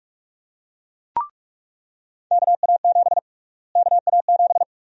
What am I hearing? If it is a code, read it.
Morse code, heard as a keyed tone, about 46 wpm, 700 Hz. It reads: KA7 KA7
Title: USA Call Sign Prefixes - Repeated 2x - 45wpm